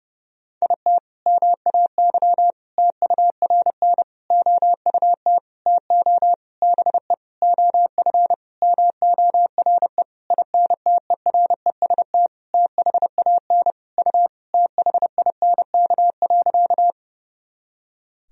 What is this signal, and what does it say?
Morse code, heard as a keyed tone, about 32 wpm, 700 Hz. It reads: IT MAY TURN OUT TO BE OF MORE INTEREST THAN U THINK.
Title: Sentences from Top 400 Words and Abbreviations 30wpm